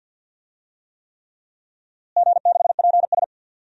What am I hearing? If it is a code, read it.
Morse code, heard as a keyed tone, about 51 wpm, 700 Hz. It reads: G6PS
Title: DXCC-100 Call Signs - 4 Characters - Repeated 3x - 50wpm